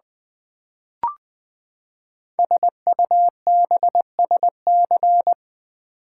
Morse code, heard as a keyed tone, about 20 wpm, 700 Hz. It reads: SUBSC
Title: Build A Word - 5 Letters At A Time - 10-Letter Words - 20wpm